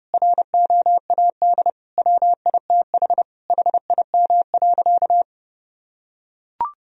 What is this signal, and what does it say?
Morse code, heard as a keyed tone, about 31 wpm, 700 Hz. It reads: ROAD WITH HIM.
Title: Sentences from Top 1000 Words - Review Part 2 - 30wpm